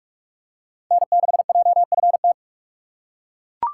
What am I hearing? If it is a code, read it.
Morse code, heard as a keyed tone, about 42 wpm, 700 Hz. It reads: N6JFT
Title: USA Call Signs - 5 Characters - 4x Character Spacing - 45wpm